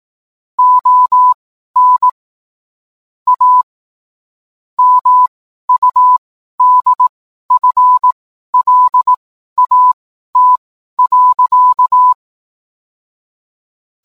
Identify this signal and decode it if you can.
Morse code, heard as a keyed tone, about 18 wpm, spaced as for 12 wpm, 1000 Hz. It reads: ON A MUDFLAT.